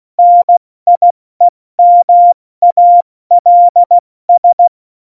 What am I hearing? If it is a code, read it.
Morse code, heard as a keyed tone, about 16 wpm, 700 Hz. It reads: NIEMALS
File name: Morsekurs / Witze / Witz 11. Juni 2019, 70 F80